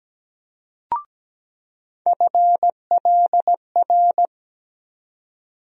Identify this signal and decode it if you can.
Morse code, heard as a keyed tone, about 17 wpm, 700 Hz. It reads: FLR
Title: USA Call Sign Postfixes - 3 Letters 17wpm